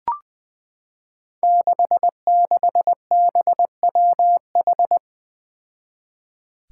Morse code, heard as a keyed tone, about 20 wpm, 700 Hz. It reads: 66BWH
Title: Five Challenging Letter-Number Combinations 20wpm